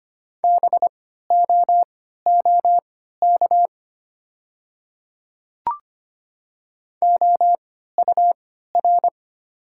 Morse code, heard as a keyed tone, about 25 wpm, spaced as for 14 wpm, 700 Hz. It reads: BOOK OUR
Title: 3x Character Spacing - Top 300 Words - 25wpm